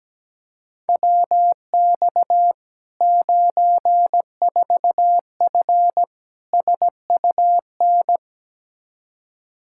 Morse code, heard as a keyed tone, about 17 wpm, 700 Hz. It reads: WX 94F SUN
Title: Practice QSO - Exchange 2 - Info - 17wpm